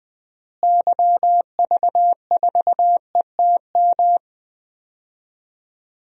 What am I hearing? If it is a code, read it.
Morse code, heard as a keyed tone, about 20 wpm, 700 Hz. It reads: YV4ETM
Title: DXCC-100 Call Signs - 6 Characters - Repeated 3x - 20wpm